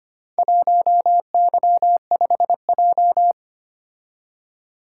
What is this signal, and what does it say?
Morse code, heard as a keyed tone, about 25 wpm, 700 Hz. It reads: 1Y5J
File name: practice / 25wpm / Four Letter-Number Combinations - 25wpm